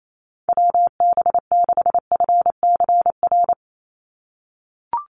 Morse code, heard as a keyed tone, about 28 wpm, 700 Hz. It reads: WB6FCR
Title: USA Call Signs 28wpm